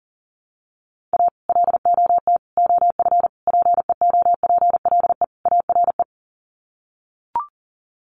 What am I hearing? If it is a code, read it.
Morse code, heard as a keyed tone, about 40 wpm, 700 Hz. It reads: A LOT OF PEOPLE ARE
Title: Sets of 5 Words 40wpm